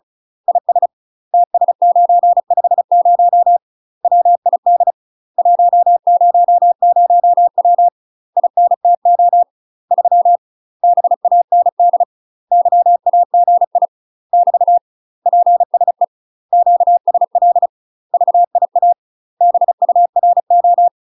Morse code, heard as a keyed tone, about 34 wpm, 700 Hz. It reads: IS TS950 WID 100W INTO 3 BAND YAGI <BT> PSE QSL VIA BURO
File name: UK MASTER QSOs TRAINING / 35WPM / UK MASTER QSOs 0042